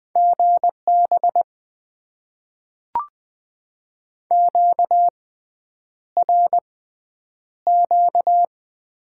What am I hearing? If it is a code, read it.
Morse code, heard as a keyed tone, about 20 wpm, spaced as for 9 wpm, 700 Hz. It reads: GB QRQ
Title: Top 90 QSO Elements - 6x Character Spacing - 20wpm